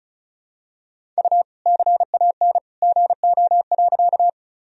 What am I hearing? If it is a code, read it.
Morse code, heard as a keyed tone, about 36 wpm, 700 Hz. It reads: U CAN GO.
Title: Sentences from Top 100 Words and Abbreviations 35wpm